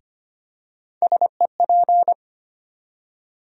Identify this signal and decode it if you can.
Morse code, heard as a keyed tone, about 25 wpm, 700 Hz. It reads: SEP
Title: USA Call Sign Postfixes - 3 Letters 25wpm